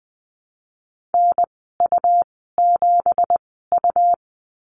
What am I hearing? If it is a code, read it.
Morse code, heard as a keyed tone, about 20 wpm, spaced as for 15 wpm, 700 Hz. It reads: NU7U
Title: 173 Lesson 28g - Number 9 - Call Signs 20wpm